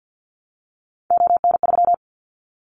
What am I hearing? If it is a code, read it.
Morse code, heard as a keyed tone, about 48 wpm, 700 Hz. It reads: ON3
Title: Callsign Prefixes for Top 100 DXCC - 50wpm